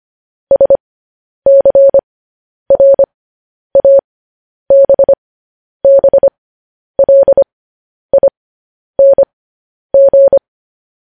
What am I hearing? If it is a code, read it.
Morse code, heard as a keyed tone, about 25 wpm, spaced as for 10 wpm, 550 Hz. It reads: SCRABBLING
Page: news-headlines-in-morse-code-at-10-wpm